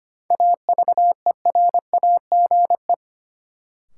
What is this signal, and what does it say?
Morse code, heard as a keyed tone, about 25 wpm, 700 Hz. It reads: AVERAGE
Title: Sets of 5 Words - New Word Drill - 25wpm